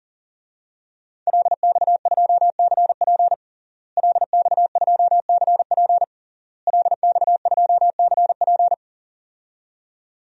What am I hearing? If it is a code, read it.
Morse code, heard as a keyed tone, about 37 wpm, 700 Hz. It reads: LX2CP LX2CP LX2CP
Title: DXCC-100 Call Signs - 5 Characters - Repeated 3x - 40wpm